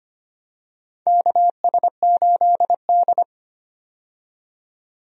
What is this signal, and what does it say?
Morse code, heard as a keyed tone, about 25 wpm, 700 Hz. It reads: KS8D
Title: USA Call Signs - 4 Characters - 25wpm